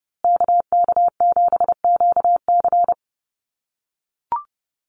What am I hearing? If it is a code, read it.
Morse code, heard as a keyed tone, about 30 wpm, 700 Hz. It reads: KK7QC
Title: USA Call Signs Repeated 2x 30wpm